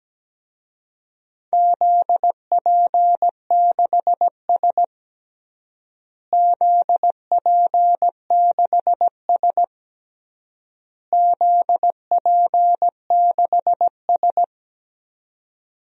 Morse code, heard as a keyed tone, about 17 wpm, 700 Hz. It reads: ZP6S ZP6S ZP6S
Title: DXCC-100 Call Signs - 4 Characters - Repeated 3x - 17wpm